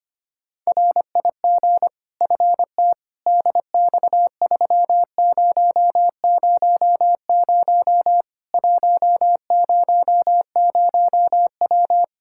Morse code, heard as a keyed tone, about 25 wpm, 700 Hz. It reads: RIG FT DX3000 100W
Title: K1USN SST QSO Practice 25wpm